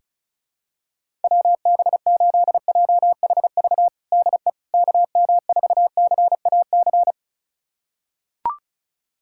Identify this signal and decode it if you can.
Morse code, heard as a keyed tone, about 36 wpm, 700 Hz. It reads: WB8JHV DE KM4CAC
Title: USA Callsign DE Callsign 35wpm